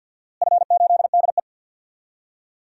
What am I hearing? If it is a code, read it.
Morse code, heard as a keyed tone, about 47 wpm, 700 Hz. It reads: F8DE
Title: DXCC-100 Call Signs - 4 Characters - Repeated 3x - 50wpm